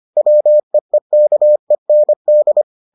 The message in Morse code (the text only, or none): WEEKEND